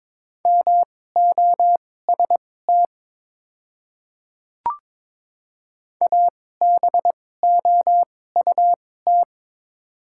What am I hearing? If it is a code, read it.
Morse code, heard as a keyed tone, about 22 wpm, spaced as for 16 wpm, 700 Hz. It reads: MOST ABOUT
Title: 2x Character Spacing - Top 100 Words - 22wpm